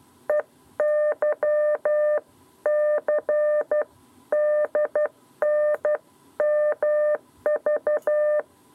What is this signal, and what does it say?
Morse code, heard as a keyed tone, about 12 wpm, 600 Hz. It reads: EYC DNMV
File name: 7b lezione cw 2